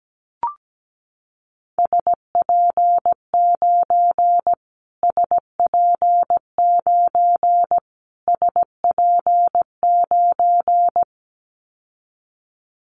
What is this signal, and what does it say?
Morse code, heard as a keyed tone, about 17 wpm, 700 Hz. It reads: SP9 SP9 SP9
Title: Callsign Prefixes for Top 25 DXCC - Repeated 3x - 17wpm